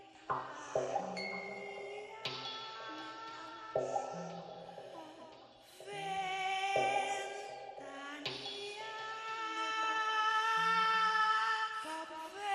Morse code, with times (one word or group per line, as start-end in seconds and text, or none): none